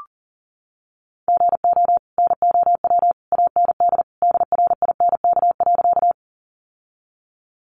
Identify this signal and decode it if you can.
Morse code, heard as a keyed tone, about 40 wpm, 700 Hz. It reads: GO NOW AND DRINK.
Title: Sentences from Top 1500 Words 40wpm